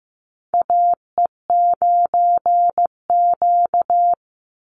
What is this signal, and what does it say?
Morse code, heard as a keyed tone, about 15 wpm, 700 Hz. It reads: AE9Q
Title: NA SOTA Chaser Callsigns 15wpm